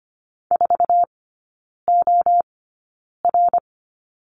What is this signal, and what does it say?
Morse code, heard as a keyed tone, about 25 wpm, spaced as for 9 wpm, 700 Hz. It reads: 4OR
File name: Lesson 14e - Number 2 - Three Characters 25wpm